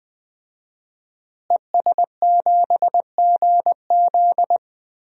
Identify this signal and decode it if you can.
Morse code, heard as a keyed tone, about 20 wpm, 700 Hz. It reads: ES7GZ
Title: DXCC-100 Call Signs - 5 Characters - Repeated 2x - 20wpm